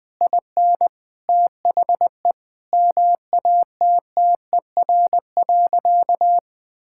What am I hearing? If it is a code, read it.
Morse code, heard as a keyed tone, about 20 wpm, 700 Hz. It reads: IN THE MATTER.
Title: Sentences from Top 1000 Words - Review Part 6 - 20wpm